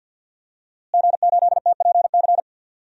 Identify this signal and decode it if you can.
Morse code, heard as a keyed tone, about 52 wpm, 700 Hz. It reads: G8TPC
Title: DXCC-100 Call Signs - 5 Characters - Repeated 3x - 50wpm